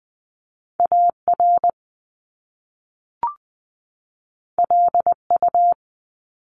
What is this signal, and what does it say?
Morse code, heard as a keyed tone, about 20 wpm, 700 Hz. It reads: AR LU
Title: Top Two Letter Combinations in Words 20wpm